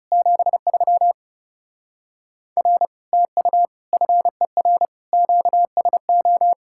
Morse code, heard as a keyed tone, about 35 wpm, 700 Hz. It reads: 73 R TU FER QSO
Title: Speed-Racing Common QSO Phrases Encore - 30wpm